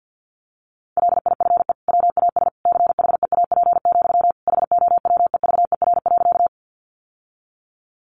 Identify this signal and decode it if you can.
Morse code, heard as a keyed tone, about 50 wpm, 700 Hz. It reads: LIFE WAS CHEAP, HOWEVER.
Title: Sentences from Top 1200 Words 50wpm